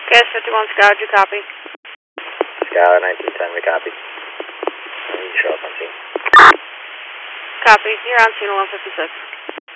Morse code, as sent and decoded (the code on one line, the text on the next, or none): none